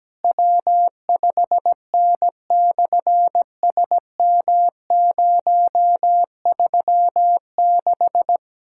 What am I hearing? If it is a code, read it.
Morse code, heard as a keyed tone, about 18 wpm, 700 Hz. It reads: W5N/SM036
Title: SOTA QSO Practice 17wpm